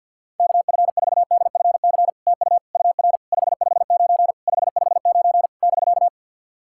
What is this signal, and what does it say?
Morse code, heard as a keyed tone, about 50 wpm, 700 Hz. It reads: KF4DFC TU UR 559 559 <BK>
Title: SOTA QSO Practice 50wpm